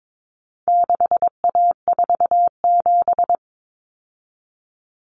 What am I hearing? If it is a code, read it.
Morse code, heard as a keyed tone, about 22 wpm, 700 Hz. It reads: 6A47
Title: Common Cut Numbers - 4-Digit 22wpm